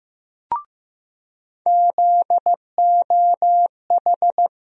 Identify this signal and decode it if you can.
Morse code, heard as a keyed tone, about 15 wpm, 700 Hz. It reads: ZOH
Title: USA Call Sign Postfixes - 3 Letters 15wpm